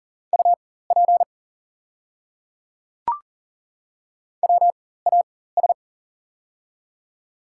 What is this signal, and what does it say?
Morse code, heard as a keyed tone, about 38 wpm, spaced as for 18 wpm, 700 Hz. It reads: UP WAS
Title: 4x Character Spacing - Top 100 Words - 40wpm